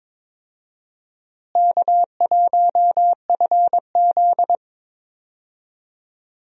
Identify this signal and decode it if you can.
Morse code, heard as a keyed tone, about 22 wpm, 700 Hz. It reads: K1FZ
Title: USA Call Signs - 4 Characters - Repeated 3x - 22wpm